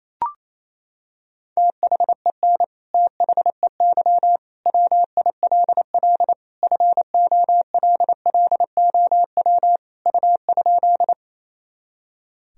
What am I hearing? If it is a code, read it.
Morse code, heard as a keyed tone, about 28 wpm, 700 Hz. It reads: THEN THEY WILL FOLLOW U?
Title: Sentences from Top 200 Words and Abbreviations 28wpm